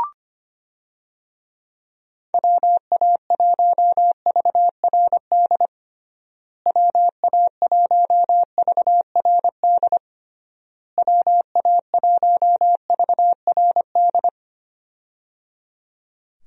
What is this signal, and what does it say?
Morse code, heard as a keyed tone, about 25 wpm, 700 Hz. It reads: WA1VRD WA1VRD WA1VRD
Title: USA Call Signs - 6 Characters - Repeated 3x - 25wpm